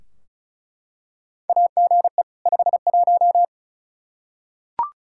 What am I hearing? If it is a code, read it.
Morse code, heard as a keyed tone, about 36 wpm, 700 Hz. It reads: AGE 51
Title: Practice QSO - Exchange 3 - More Info - 35wpm